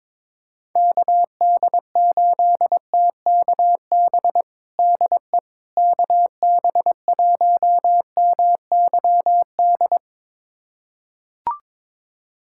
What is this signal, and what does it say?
Morse code, heard as a keyed tone, about 22 wpm, 700 Hz. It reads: KD8TKB DE KB1MYD
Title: USA Callsign DE Callsign - 6 Characters - 22wpm